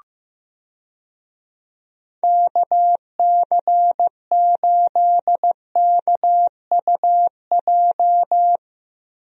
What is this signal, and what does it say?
Morse code, heard as a keyed tone, about 15 wpm, 700 Hz. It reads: KC8KUJ
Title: USA Call Signs - 6 Characters - Repeated 3x - 15wpm